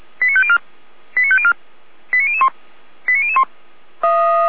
Five-tone selective call (ZVEI), heard as quick runs of five tones, tones of about 70 ms. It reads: 89504 89504 89001 89001